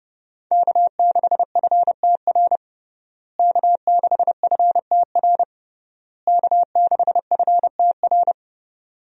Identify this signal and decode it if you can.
Morse code, heard as a keyed tone, about 30 wpm, 700 Hz. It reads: K6FTR K6FTR K6FTR
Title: USA Call Signs - 5 Characters - Repeated 3x - 30wpm